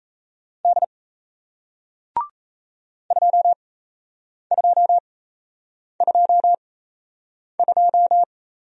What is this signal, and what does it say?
Morse code, heard as a keyed tone, about 38 wpm, spaced as for 8 wpm, 700 Hz. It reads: D 2222